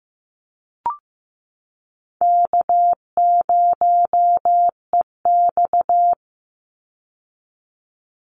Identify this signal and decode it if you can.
Morse code, heard as a keyed tone, about 15 wpm, 700 Hz. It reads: K0EX
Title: Call Signs Repeated 2x - 15wpm